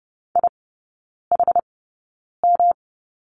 Morse code, heard as a keyed tone, about 30 wpm, spaced as for 9 wpm, 700 Hz. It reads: IHM
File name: Lesson 18e - Letter m - Three Characters 30wpm